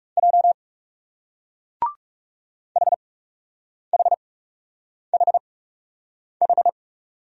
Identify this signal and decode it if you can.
Morse code, heard as a keyed tone, about 41 wpm, spaced as for 8 wpm, 700 Hz. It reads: J HHHH